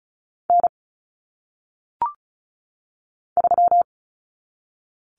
Morse code, heard as a keyed tone, about 36 wpm, 700 Hz. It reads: N 3
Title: Common Cut Numbers - 1-Digit 35wpm